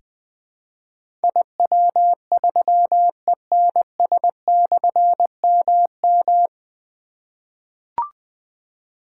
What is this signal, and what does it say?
Morse code, heard as a keyed tone, about 20 wpm, 700 Hz. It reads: IW3ENS/MM
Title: Difficult Call Signs 20wpm